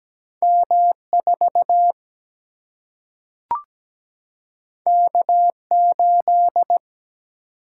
Morse code, heard as a keyed tone, about 17 wpm, 700 Hz. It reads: M 4 K 8
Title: Single-Letter Single-Number 17wpm